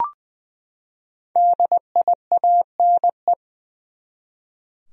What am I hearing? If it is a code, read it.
Morse code, heard as a keyed tone, about 20 wpm, 700 Hz. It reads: DIANE
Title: USA Common Names - 20wpm